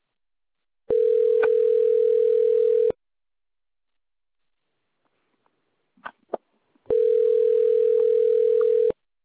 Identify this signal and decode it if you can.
Ringback tone (North America), 440 + 480 Hz.